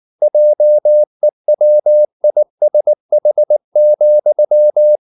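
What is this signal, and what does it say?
Morse code, heard as a keyed tone, about 19 wpm, 600 Hz. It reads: JEWISH,
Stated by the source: Morse code at 19 wpm from CW med Gnister, reading Ken Follett's novel Never in English.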